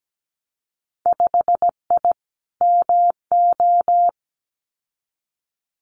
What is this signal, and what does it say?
Morse code, heard as a keyed tone, about 17 wpm, 700 Hz. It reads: 5I MO
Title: Winter Field Day Exchanges Encore 17wpm